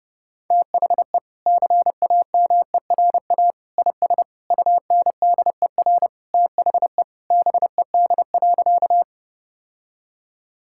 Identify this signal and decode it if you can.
Morse code, heard as a keyed tone, about 30 wpm, 700 Hz. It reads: THE CAMERA IS UNDER THE BED.
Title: Sentences from Top 1000 Words - Review Part 5 - 30wpm